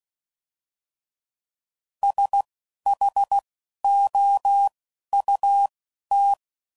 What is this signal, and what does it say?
Morse code, heard as a keyed tone, about 16 wpm, spaced as for 11 wpm, 800 Hz. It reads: SHOUT
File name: Prefixes / By First Letter / S vs H